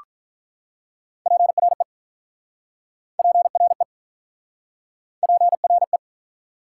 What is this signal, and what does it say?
Morse code, heard as a keyed tone, about 48 wpm, 700 Hz. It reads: PRE PRE PRE